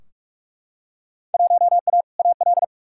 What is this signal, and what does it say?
Morse code, heard as a keyed tone, about 44 wpm, 700 Hz. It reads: 1A AL
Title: Field Day Exchanges - 45wpm